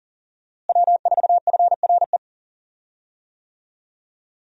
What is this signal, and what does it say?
Morse code, heard as a keyed tone, about 40 wpm, 700 Hz. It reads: W4FRE
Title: USA Call Signs - 5 Characters - Repeated 3x - 40wpm